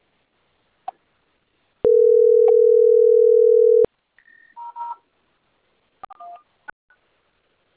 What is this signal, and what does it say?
Ringback tone (North America), 440 + 480 Hz.